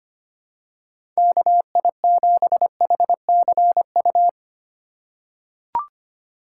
Morse code, heard as a keyed tone, about 25 wpm, 700 Hz. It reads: KI7HCU
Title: USA Call Signs - 6 Characters - 3x Character Spacing - 25wpm